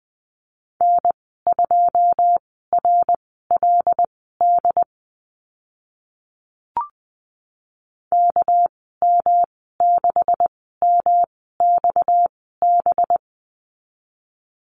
Morse code, heard as a keyed tone, about 20 wpm, spaced as for 14 wpm, 700 Hz. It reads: N2RLD KM6MXB
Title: 245 Lesson 39g - Letter z - Call Signs 20wpm